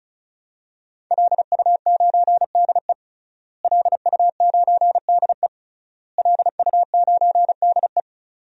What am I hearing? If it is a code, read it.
Morse code, heard as a keyed tone, about 36 wpm, 690 Hz. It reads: LU9DE LU9DE LU9DE
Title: DXCC-100 Call Signs - 5 Characters - Repeated 3x - 35wpm